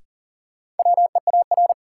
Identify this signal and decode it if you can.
Morse code, heard as a keyed tone, about 40 wpm, 700 Hz. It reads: WEAR
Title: Sentences from Top 1000 Words - Review Part 6 - New Word Drill - 40wpm